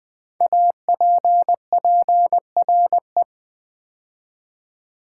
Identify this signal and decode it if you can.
Morse code, heard as a keyed tone, about 20 wpm, 700 Hz. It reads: APPRE
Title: Build A Word - 5 Letters At A Time - 12-Letter Words - 20wpm